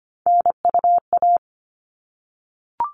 Morse code, heard as a keyed tone, about 25 wpm, 700 Hz. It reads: NUA
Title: All Cut Numbers - 3-Digit 25wpm